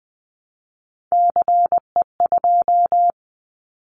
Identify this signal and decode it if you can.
Morse code, heard as a keyed tone, about 20 wpm, 700 Hz. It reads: CE2
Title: Callsign Prefixes for Top 100 DXCC - 20wpm